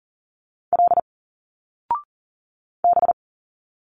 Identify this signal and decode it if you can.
Morse code, heard as a keyed tone, about 40 wpm, 700 Hz. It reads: L B
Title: Single Letters 40wpm